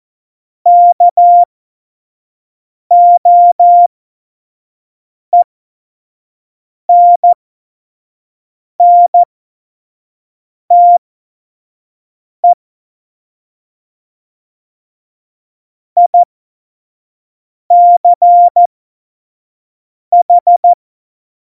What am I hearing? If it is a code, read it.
Morse code, heard as a keyed tone, about 14 wpm, spaced as for 5 wpm, 700 Hz. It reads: KOENNTE ICH
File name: Morsekurs / Witze / Witz 05. Januar 2021, 25 F70